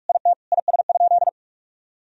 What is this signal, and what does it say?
Morse code, heard as a keyed tone, about 41 wpm, 700 Hz. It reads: IT IS?
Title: Sentences from Top 1000 Words - Review Part 1 - 45wpm